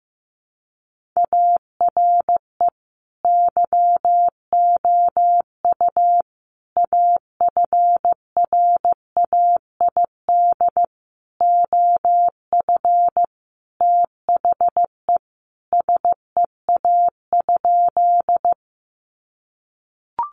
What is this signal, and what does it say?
Morse code, heard as a keyed tone, about 15 wpm, 700 Hz. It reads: ARE YOU AFRAID OF THE SEA?
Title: Sentences from Top 1500 Words 15wpm